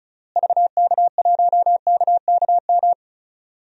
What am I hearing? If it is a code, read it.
Morse code, heard as a keyed tone, about 34 wpm, 700 Hz. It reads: VK1KKM
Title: DXCC-100 Call Signs - 6 Characters - Repeated 3x - 35wpm